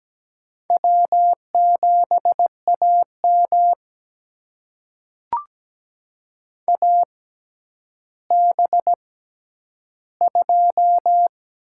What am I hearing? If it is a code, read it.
Morse code, heard as a keyed tone, about 17 wpm, spaced as for 10 wpm, 700 Hz. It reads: W7AM AB2
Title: USA Call Signs - 4 Characters - 6x Character Spacing - 17wpm